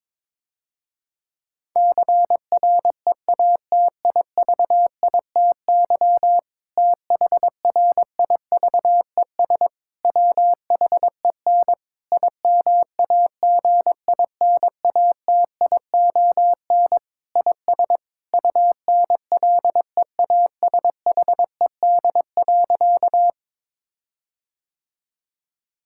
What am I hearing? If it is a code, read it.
Morse code, heard as a keyed tone, about 22 wpm, 700 Hz. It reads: CREATIVITY THRIVES WHEN IMAGINATION IS UNLEASHED.
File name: Difficult Sentences with Hints - 22wpm